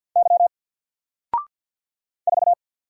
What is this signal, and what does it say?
Morse code, heard as a keyed tone, about 48 wpm, 700 Hz. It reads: Y 4
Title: Single Character (Letter:Number:Punctuation:Prosign) Alt 50wpm